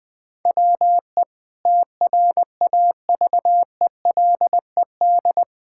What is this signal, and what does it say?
Morse code, heard as a keyed tone, about 20 wpm, 700 Hz. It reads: WE TRAVELED
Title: Sentences with 2-Word Phrases - Spoken Beginning - 20wpm